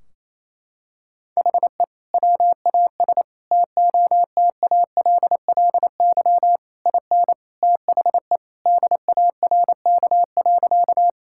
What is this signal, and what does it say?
Morse code, heard as a keyed tone, about 28 wpm, 700 Hz. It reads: HE WAS TOTALLY IN THE DARK.
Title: Sentences from Top 1600 Words 28wpm